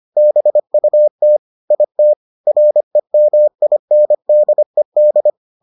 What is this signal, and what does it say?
Morse code, heard as a keyed tone, about 25 wpm, 600 Hz. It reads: BUT IT REMINDED